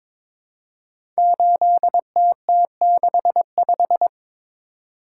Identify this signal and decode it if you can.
Morse code, heard as a keyed tone, about 22 wpm, 700 Hz. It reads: 8TT65
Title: Code Groups - 5 Letter:Numbers - 22wpm